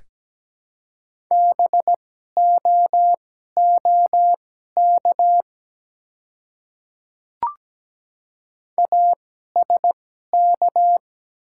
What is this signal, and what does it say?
Morse code, heard as a keyed tone, about 17 wpm, spaced as for 12 wpm, 690 Hz. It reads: BOOK ASK
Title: 2x Character Spacing - Top 300 Words - 17wpm